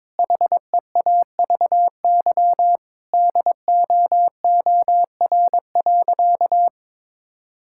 Morse code, heard as a keyed tone, about 22 wpm, 700 Hz. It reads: HEAVY DOOR.
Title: Sentences from Top 1600 Words 22wpm